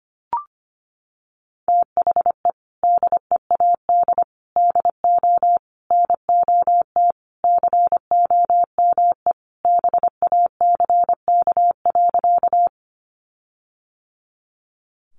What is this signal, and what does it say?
Morse code, heard as a keyed tone, about 25 wpm, 700 Hz. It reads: THE DEAD DO NOT COME BACK.
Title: Sentences from Top 1500 Words 25wpm